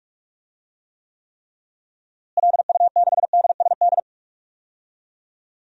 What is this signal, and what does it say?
Morse code, heard as a keyed tone, about 41 wpm, 700 Hz. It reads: LU6DSD
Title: DXCC-100 Call Signs - 6 Characters - Repeated 3x - 45wpm